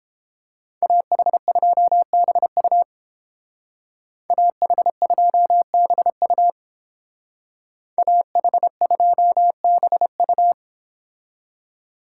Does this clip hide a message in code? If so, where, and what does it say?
Yes: Morse code from 0.8 to 10.5 s, heard as a keyed tone, about 29 wpm, 700 Hz. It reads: AH2BU AH2BU AH2BU